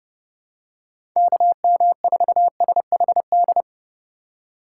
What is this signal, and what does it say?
Morse code, heard as a keyed tone, about 32 wpm, 700 Hz. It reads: KM4SHD
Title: USA Call Signs - 3x Character Spacing - 30wpm